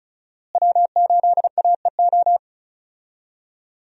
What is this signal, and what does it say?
Morse code, heard as a keyed tone, about 36 wpm, 700 Hz. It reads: W8AEO
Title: USA Call Signs - 5 Characters - Repeated 2x - 35wpm